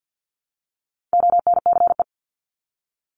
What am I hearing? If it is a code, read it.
Morse code, heard as a keyed tone, about 50 wpm, 700 Hz. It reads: ONCE
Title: Top 500 Words 50wpm